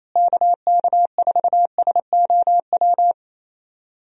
Morse code, heard as a keyed tone, about 27 wpm, 700 Hz. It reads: KK4SOW
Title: USA Call Signs - 6 Characters - Repeated 2x - 28wpm